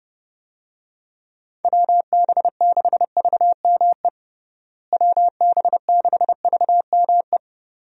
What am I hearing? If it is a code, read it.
Morse code, heard as a keyed tone, about 30 wpm, 700 Hz. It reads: WB6VME WB6VME
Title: USA Call Signs - 6 Characters - Repeated 2x - 30wpm